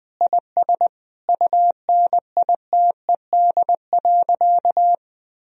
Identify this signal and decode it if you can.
Morse code, heard as a keyed tone, about 21 wpm, 700 Hz. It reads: IS UNITED.
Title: Sentences from Top 1000 Words - Review Part 4 - 20wpm